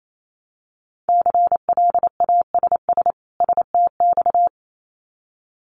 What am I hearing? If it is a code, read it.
Morse code, heard as a keyed tone, about 28 wpm, 700 Hz. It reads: CLASS STX
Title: Winter Field Day Exchanges 28wpm